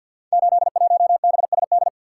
Morse code, heard as a keyed tone, about 51 wpm, 700 Hz. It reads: 81BSD
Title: Code Groups - 5 Letter:Numbers - 50wpm